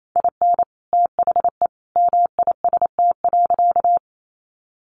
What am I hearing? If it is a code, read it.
Morse code, heard as a keyed tone, about 28 wpm, 700 Hz. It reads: IN THE MIST.
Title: A Sound Of Thunder 28wpm